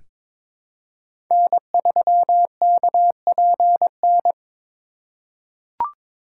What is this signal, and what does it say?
Morse code, heard as a keyed tone, about 22 wpm, 700 Hz. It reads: N3KPN
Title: USA Call Signs - 5 Characters - 3x Character Spacing - 22wpm